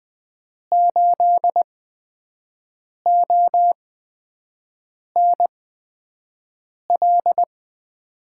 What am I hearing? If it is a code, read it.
Morse code, heard as a keyed tone, about 20 wpm, spaced as for 5 wpm, 700 Hz. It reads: 8ONL
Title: USA Call Signs - 6 Characters - 8x Character Spacing - 20wpm